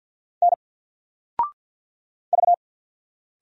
Morse code, heard as a keyed tone, about 48 wpm, 700 Hz. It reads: N V
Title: Single Letter-Number Alt 50wpm